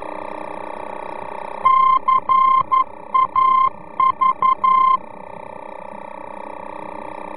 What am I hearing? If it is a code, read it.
Morse code, heard as a keyed tone, about 11 wpm, 1050 Hz. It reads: CAV
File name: france / mx / CAV-VOR-F